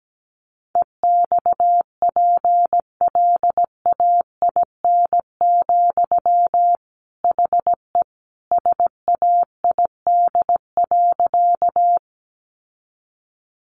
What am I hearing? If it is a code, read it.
Morse code, heard as a keyed tone, about 17 wpm, 700 Hz. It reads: EXPLAIN, HE SAID.
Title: Sentences from Top 1100 Words 17wpm